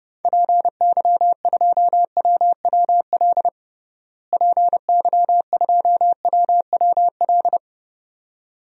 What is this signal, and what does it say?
Morse code, heard as a keyed tone, about 31 wpm, 700 Hz. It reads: PY2WWL PY2WWL
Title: DXCC-100 Call Signs - 6 Characters - Repeated 2x - 30wpm